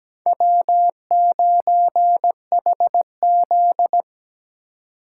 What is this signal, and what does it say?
Morse code, heard as a keyed tone, about 17 wpm, 700 Hz. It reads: W9HZ
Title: USA Call Signs - 4 Characters - 3x Character Spacing - 17wpm